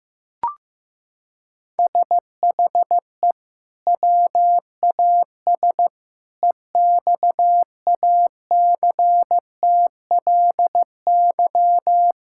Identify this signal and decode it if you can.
Morse code, heard as a keyed tone, about 15 wpm, 700 Hz. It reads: SHE WAS EXACTLY
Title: Sentences from Top 1000 Words - Review Part 6 - 15wpm